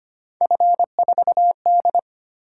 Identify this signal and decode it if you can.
Morse code, heard as a keyed tone, about 29 wpm, 700 Hz. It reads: F4D
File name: Speed-Racing - Three Letter-Number Combinations - 25wpm